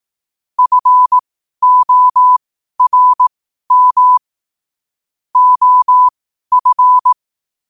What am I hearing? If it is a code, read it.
Morse code, heard as a keyed tone, about 18 wpm, spaced as for 12 wpm, 1000 Hz. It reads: FORM OF
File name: qotdfiles / 12wpm